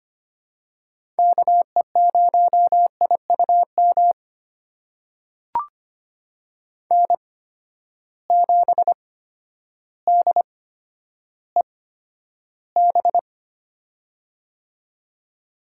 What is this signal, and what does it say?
Morse code, heard as a keyed tone, about 25 wpm, spaced as for 11 wpm, 700 Hz. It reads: KE0IUM N7DEB
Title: USA Call Signs - 8x Character Spacing - 25wpm